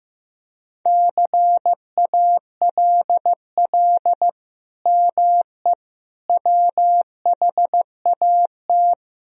Morse code, heard as a keyed tone, about 15 wpm, 700 Hz. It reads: CALL ME WHAT
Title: Sentences from Top 200 Words and Abbreviations 15wpm